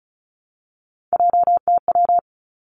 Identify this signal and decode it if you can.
Morse code, heard as a keyed tone, about 36 wpm, 700 Hz. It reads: JTW
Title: Three Challenging Letter-Number Combinations 35wpm